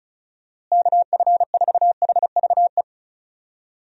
Morse code, heard as a keyed tone, about 36 wpm, 700 Hz. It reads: KF4HVE
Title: USA Call Signs - 6 Characters - Repeated 2x - 35wpm